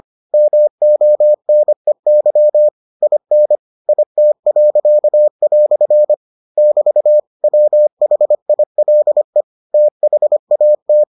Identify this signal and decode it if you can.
Morse code, heard as a keyed tone, about 25 wpm, 600 Hz. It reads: MONEY IN IT." <BT> WHILE THAT